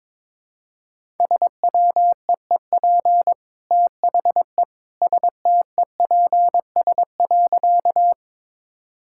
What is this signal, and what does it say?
Morse code, heard as a keyed tone, about 22 wpm, 700 Hz. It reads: SWEEP THE STEPS.
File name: practice / 22wpm / Easy Sentences with Hints - 22wpm